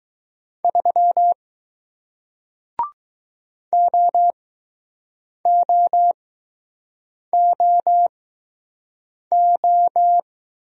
Morse code, heard as a keyed tone, about 20 wpm, spaced as for 6 wpm, 700 Hz. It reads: 3 OOOO